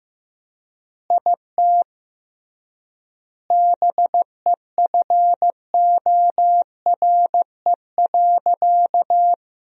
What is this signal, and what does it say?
Morse code, heard as a keyed tone, about 15 wpm, 700 Hz. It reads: IT BEFORE.
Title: 3x Word-Spacing - Sentences from Top 400 Words - 15wpm